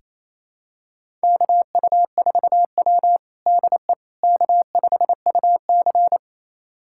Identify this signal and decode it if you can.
Morse code, heard as a keyed tone, about 28 wpm, 700 Hz. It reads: KU4W DE K5UC
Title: USA Callsign DE Callsign - 4 Characters - 28wpm